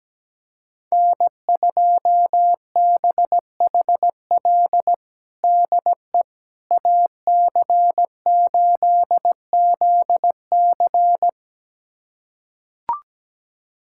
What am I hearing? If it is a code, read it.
Morse code, heard as a keyed tone, about 17 wpm, 700 Hz. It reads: N2BHL DE AC8ZC
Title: USA Callsign DE Callsign - 5 Characters - 17wpm